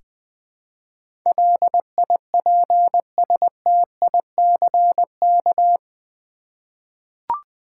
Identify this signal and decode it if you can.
Morse code, heard as a keyed tone, about 20 wpm, 700 Hz. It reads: LIPSTICK
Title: Easy Words with Hints - 20wpm